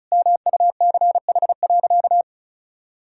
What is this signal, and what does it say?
Morse code, heard as a keyed tone, about 36 wpm, 700 Hz. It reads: MUCH.
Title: Sentences from Top 1000 Words - Review Part 1 - 35wpm